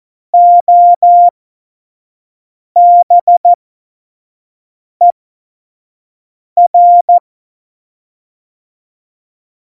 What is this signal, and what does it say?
Morse code, heard as a keyed tone, about 14 wpm, spaced as for 5 wpm, 700 Hz. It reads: OBER
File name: Morsekurs / Witze / Witz 03. September 2019, 25 F70